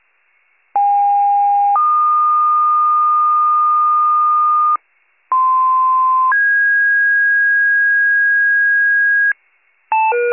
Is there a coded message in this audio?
Yes: two-tone page from 0.8 to 9.3 s, heard as a one-second tone, then a three-second tone, about 810 then 1220 Hz, about 1040 then 1650 Hz, with noise.